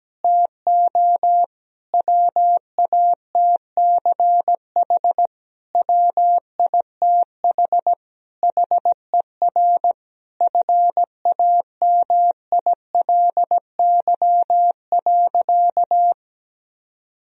Morse code, heard as a keyed tone, about 17 wpm, 700 Hz. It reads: TO WATCH WITH HER FAMILY.
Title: Sentences from Top 1000 Words - Review Part 4 - 17wpm